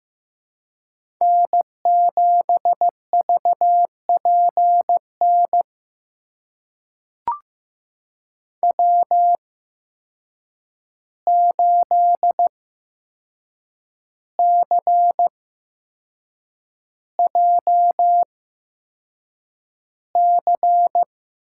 Morse code, heard as a keyed tone, about 15 wpm, spaced as for 6 wpm, 700 Hz. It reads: N7VPN W8CJC
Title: USA Call Signs - 5 Characters - 8x Character Spacing - 15wpm